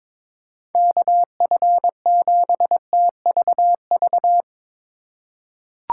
Morse code, heard as a keyed tone, about 22 wpm, 700 Hz. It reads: KF7TVV
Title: USA Call Signs - 3x Character Spacing - 22wpm